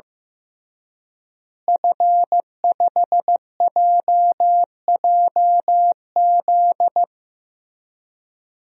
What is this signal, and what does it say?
Morse code, heard as a keyed tone, about 16 wpm, 700 Hz. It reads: F5JJZ
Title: DXCC-100 Call Signs - 5 Characters - Repeated 3x - 15wpm